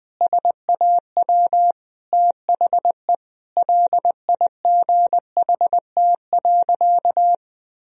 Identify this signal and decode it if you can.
Morse code, heard as a keyed tone, about 20 wpm, 700 Hz. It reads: SAW THE LIGHT.
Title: Sentences from Top 1000 Words - Review Part 1 - 20wpm